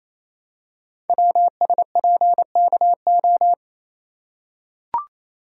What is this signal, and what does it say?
Morse code, heard as a keyed tone, about 27 wpm, 700 Hz. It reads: WSPKO
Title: Code Groups - 5 Letters - 28wpm